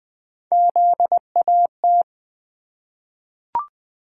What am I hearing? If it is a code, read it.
Morse code, heard as a keyed tone, about 20 wpm, 700 Hz. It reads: ZAT
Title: Three Letter-Number Combinations - 20wpm